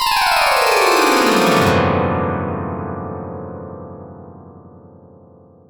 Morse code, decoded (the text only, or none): none